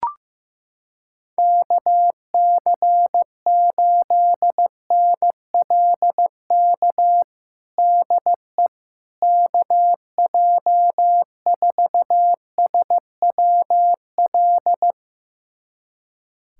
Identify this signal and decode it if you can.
Morse code, heard as a keyed tone, about 15 wpm, 700 Hz. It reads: KC8NLK DE KJ4SWL
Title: USA Callsign DE Callsign 15wpm